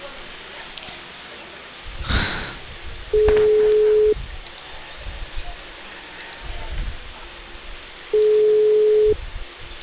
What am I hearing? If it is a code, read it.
Ringback tone (Europe, 425 Hz), with speech.